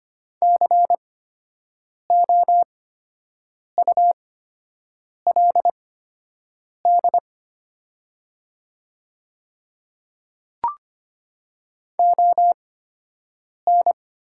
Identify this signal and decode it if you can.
Morse code, heard as a keyed tone, about 25 wpm, spaced as for 7 wpm, 700 Hz. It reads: COULD ON
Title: 8x Character Spacing - Top 200 Words - 25wpm